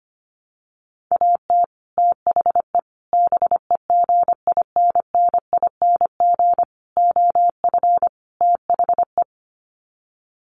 Morse code, heard as a keyed tone, about 25 wpm, 700 Hz. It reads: AT THE BEGINNING OF THE
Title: Sets of 5 Words 25wpm